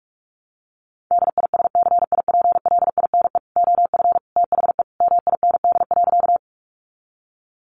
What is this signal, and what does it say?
Morse code, heard as a keyed tone, about 44 wpm, 700 Hz. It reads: DISCIPLINE OF THE MIND.
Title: Sentences from Top 900 Words 45wpm